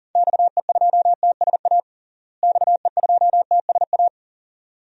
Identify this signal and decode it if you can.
Morse code, heard as a keyed tone, about 40 wpm, 700 Hz. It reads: XE2TSA XE2TSA
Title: DXCC-100 Call Signs - 6 Characters - Repeated 2x - 40wpm